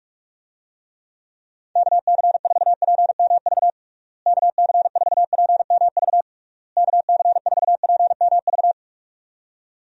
Morse code, heard as a keyed tone, about 46 wpm, 700 Hz. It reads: KC4PMV KC4PMV KC4PMV
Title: USA Call Signs - 6 Characters - Repeated 3x - 45wpm